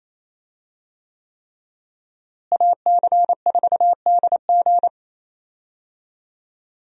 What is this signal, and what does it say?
Morse code, heard as a keyed tone, about 28 wpm, 700 Hz. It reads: AC4DG
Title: USA Call Signs - 5 Characters - Repeated 3x - 28wpm